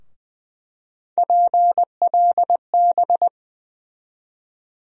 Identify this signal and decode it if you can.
Morse code, heard as a keyed tone, about 20 wpm, 700 Hz. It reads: PLB
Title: USA Call Sign Postfixes - 3 Letters 20wpm